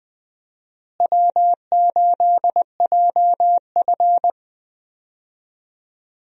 Morse code, heard as a keyed tone, about 20 wpm, 700 Hz. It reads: W8JF
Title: USA Call Signs - 4 Characters - Repeated 2x - 20wpm